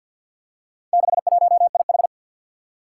Morse code, heard as a keyed tone, about 52 wpm, 700 Hz. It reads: 61IH
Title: Four Challenging Letter-Number Combinations 50wpm